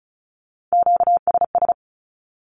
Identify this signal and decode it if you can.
Morse code, heard as a keyed tone, about 34 wpm, 700 Hz. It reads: QSS
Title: Q-Codes 35wpm